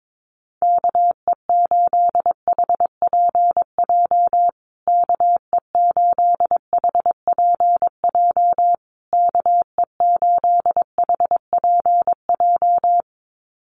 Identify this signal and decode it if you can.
Morse code, heard as a keyed tone, about 22 wpm, 700 Hz. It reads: KE8HPJ KE8HPJ KE8HPJ
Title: USA Call Signs Repeated 3x 22wpm